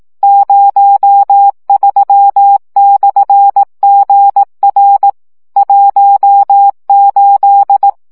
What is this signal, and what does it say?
Morse code, heard as a keyed tone, about 18 wpm, 800 Hz. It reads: 03/GR 18
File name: morze vizsga 2011-3 15